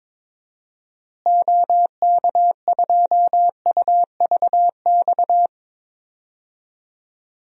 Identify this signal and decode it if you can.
Morse code, heard as a keyed tone, about 22 wpm, 700 Hz. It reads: OK2UVX
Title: DXCC-100 Call Signs - 6 Characters - Repeated 3x - 22wpm